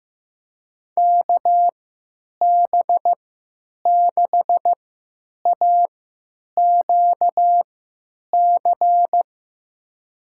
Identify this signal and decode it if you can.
Morse code, heard as a keyed tone, about 16 wpm, spaced as for 9 wpm, 700 Hz. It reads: KB6AQC
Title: USA Call Signs - 3x Character Spacing - 15wpm